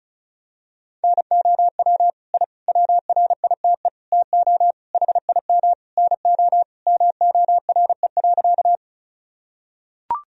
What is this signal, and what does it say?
Morse code, heard as a keyed tone, about 36 wpm, 700 Hz. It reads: NOW I WRITE TO HIM NO MORE.
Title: Sentences from Top 1000 Words - Review Part 1 - 35wpm